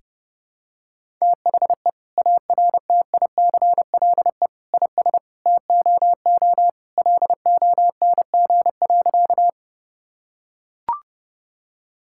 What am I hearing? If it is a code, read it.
Morse code, heard as a keyed tone, about 32 wpm, 700 Hz. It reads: THE ARTICLE IS TOO LONG.
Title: Sentences from Top 1000 Words - Review Part 5 - 30wpm